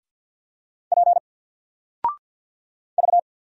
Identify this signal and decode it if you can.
Morse code, heard as a keyed tone, about 50 wpm, 700 Hz. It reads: P V
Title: Single Letter-Number 50wpm